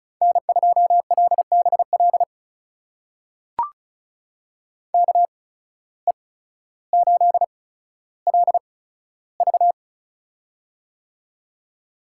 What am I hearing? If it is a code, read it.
Morse code, heard as a keyed tone, about 36 wpm, spaced as for 15 wpm, 700 Hz. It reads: N2LBL KE8LV
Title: USA Call Signs - 5 Characters - 8x Character Spacing - 35wpm